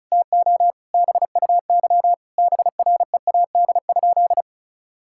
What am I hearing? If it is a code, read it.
Morse code, heard as a keyed tone, about 36 wpm, 700 Hz. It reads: TO BUY BREAD?
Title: Sentences from Top 1600 Words 35wpm